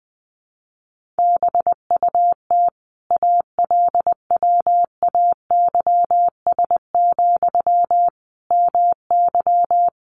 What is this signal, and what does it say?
Morse code, heard as a keyed tone, about 20 wpm, 700 Hz. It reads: BUT ALWAYS, MY
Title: Sentences from Top 1200 Words 20wpm